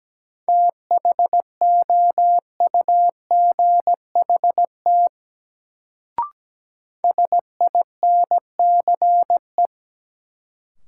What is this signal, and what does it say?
Morse code, heard as a keyed tone, about 17 wpm, 700 Hz. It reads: THOUGHT SINCE
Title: Sets of 2 Words - New Word Drill - 17wpm